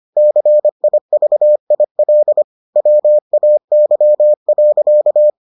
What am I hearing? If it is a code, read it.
Morse code, heard as a keyed tone, about 25 wpm, 600 Hz. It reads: CIVIL WAY.